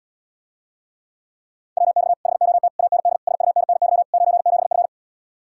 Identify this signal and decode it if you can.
Morse code, heard as a keyed tone, about 80 wpm, 700 Hz. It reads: WX HOT AND SUNNY 97F